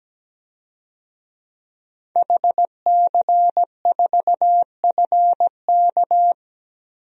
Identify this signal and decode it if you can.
Morse code, heard as a keyed tone, about 17 wpm, 700 Hz. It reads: HC4FK
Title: DXCC-100 Call Signs - 5 Characters - 17wpm